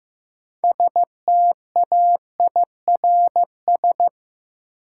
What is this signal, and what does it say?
Morse code, heard as a keyed tone, about 15 wpm, 700 Hz. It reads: STAIRS
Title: Sets of 5 Words Encore - New Word Drill - 15wpm